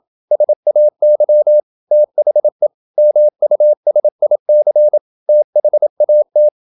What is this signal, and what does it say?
Morse code, heard as a keyed tone, about 27 wpm, 600 Hz. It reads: SAY THE MUSIC THAT
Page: Never 26wpm | CW med Gnister